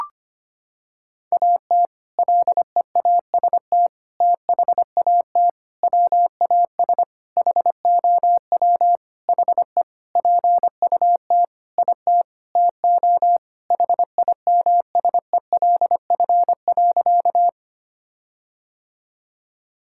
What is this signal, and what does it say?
Morse code, heard as a keyed tone, about 25 wpm, 700 Hz. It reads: AT LEAST THAT WAS HOW HE PUT IT TO HIMSELF.
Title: Sentences from Top 1600 Words 25wpm